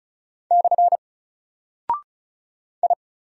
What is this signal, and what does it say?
Morse code, heard as a keyed tone, about 35 wpm, 700 Hz. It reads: / I
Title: Single Character (Letter:Number:Punctuation:Prosign) Alt 35wpm